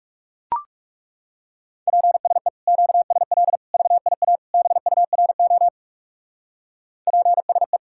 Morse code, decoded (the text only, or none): PSE QSL VIA BURO PSE